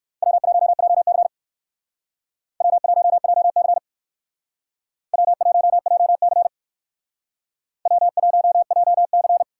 Morse code, heard as a keyed tone, about 58 wpm, 700 Hz. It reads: W1JC W1JC W1JC W1JC